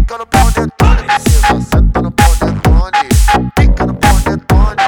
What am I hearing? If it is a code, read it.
There is no code.